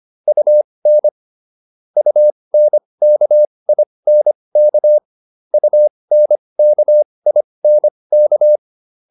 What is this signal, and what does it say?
Morse code, heard as a keyed tone, about 25 wpm, spaced as for 20 wpm, 600 Hz. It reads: UN UNKINK UNKINK
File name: UN6 20